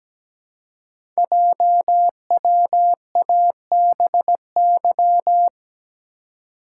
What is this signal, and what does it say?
Morse code, heard as a keyed tone, about 17 wpm, 700 Hz. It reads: JWABY
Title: Code Groups - 5 Letters - 17wpm